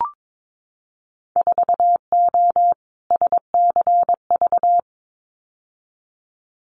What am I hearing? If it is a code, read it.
Morse code, heard as a keyed tone, about 22 wpm, 700 Hz. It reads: 4O SCV
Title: Winter Field Day Exchanges 22wpm